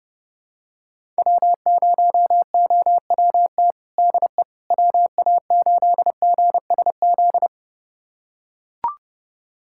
Morse code, heard as a keyed tone, about 32 wpm, 700 Hz. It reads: W0OWT DE WA8GSZ
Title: USA Callsign DE Callsign 30wpm